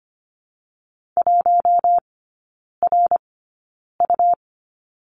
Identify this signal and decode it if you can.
Morse code, heard as a keyed tone, about 25 wpm, 700 Hz. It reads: 1 R U
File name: course / 25:10wpm / 100 Lesson 17e - Letter u - Three Characters 25wpm